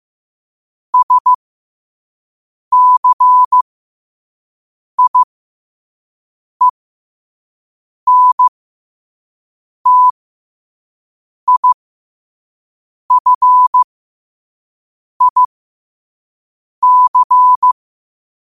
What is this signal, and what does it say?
Morse code, heard as a keyed tone, about 15 wpm, spaced as for 5 wpm, 1000 Hz. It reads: SCIENTIFIC
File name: qotdfiles / 5wpm